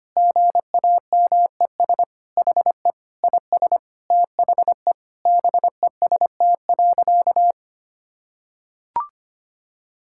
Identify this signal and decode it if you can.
Morse code, heard as a keyed tone, about 25 wpm, 700 Hz. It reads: GAMES HE IS THE BEST.
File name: practice / 25wpm / Sentences with 3-Word Phrases - 25wpm